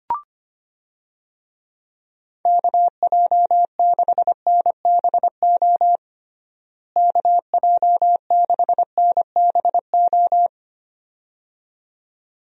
Morse code, heard as a keyed tone, about 25 wpm, 700 Hz. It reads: KJ6NBO KJ6NBO
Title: USA Call Signs - 6 Characters - Repeated 2x - 25wpm